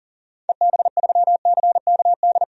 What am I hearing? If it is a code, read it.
Morse code, heard as a keyed tone, about 40 wpm, 700 Hz. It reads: EB3CKD
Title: DXCC-100 Call Signs - 6 Characters - 40wpm